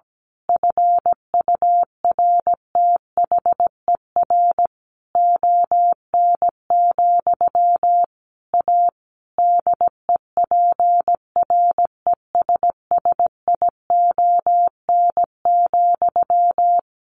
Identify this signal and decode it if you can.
Morse code, heard as a keyed tone, about 17 wpm, 700 Hz. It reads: FURTHER ON, A DEPRESSION,
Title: A Sound Of Thunder 17wpm